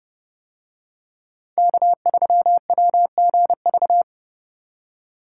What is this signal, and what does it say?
Morse code, heard as a keyed tone, about 29 wpm, 700 Hz. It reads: K3WGV